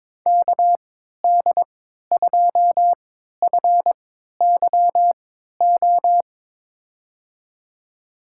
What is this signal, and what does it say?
Morse code, heard as a keyed tone, about 22 wpm, spaced as for 12 wpm, 700 Hz. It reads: KD2FYO
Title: USA Call Signs - 6 Characters - 3x Character Spacing - 22wpm